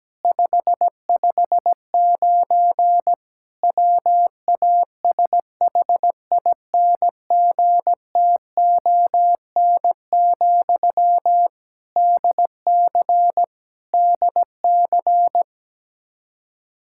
Morse code, heard as a keyed tone, about 17 wpm, 700 Hz. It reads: 559 WASHINGTON, DC DC
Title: Practice QSO - Exchange 1 - Beginning - 17wpm